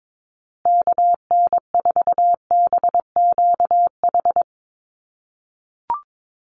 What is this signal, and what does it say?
Morse code, heard as a keyed tone, about 22 wpm, 700 Hz. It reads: KN4BQH
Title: USA Call Signs Repeated 2x 22wpm